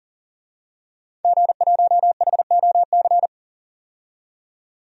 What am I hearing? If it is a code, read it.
Morse code, heard as a keyed tone, about 40 wpm, 700 Hz. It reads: G1HOC
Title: DXCC-100 Call Signs - 5 Characters - 40wpm